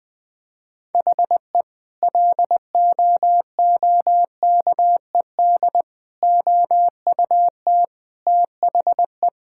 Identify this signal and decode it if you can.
Morse code, heard as a keyed tone, about 20 wpm, 700 Hz. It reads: HE LOOKED OUT THE
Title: Sentences with 2-Word Phrases - 20wpm